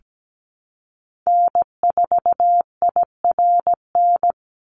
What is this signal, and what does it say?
Morse code, heard as a keyed tone, about 17 wpm, 700 Hz. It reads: N4IRN
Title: Call Signs Repeated 3x - 17wpm